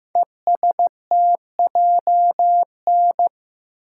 Morse code, heard as a keyed tone, about 15 wpm, 700 Hz. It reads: ESTJN
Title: Code Groups - 5 Letter:Numbers - 15wpm